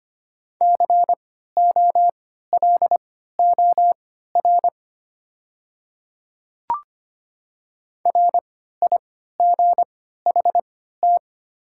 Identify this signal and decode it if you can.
Morse code, heard as a keyed tone, about 25 wpm, spaced as for 14 wpm, 700 Hz. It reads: COLOR RIGHT
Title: 3x Character Spacing - Top 300 Words - 25wpm